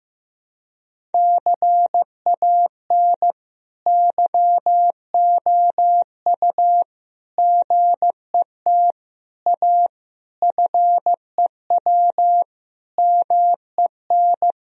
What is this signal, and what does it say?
Morse code, heard as a keyed tone, about 15 wpm, 700 Hz. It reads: CAN YOU GET A FEW MEN
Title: Sentences from Top 1000 Words - Review Part 1 - 15wpm